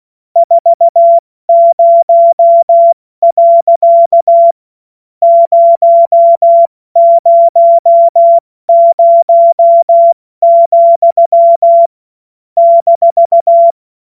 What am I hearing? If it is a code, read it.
Morse code, heard as a keyed tone, about 16 wpm, spaced as for 12 wpm, 680 Hz. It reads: 40.000,-
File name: Morsekurs / Witze / Witz 06. September 2022 70 F80